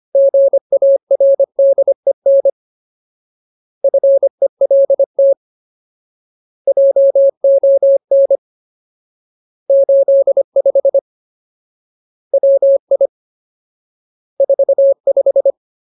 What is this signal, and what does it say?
Morse code, heard as a keyed tone, about 25 wpm, 550 Hz. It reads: GARDEN FELT JON 85 WI 45
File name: Ultimate Word List / 25WPM Ultimate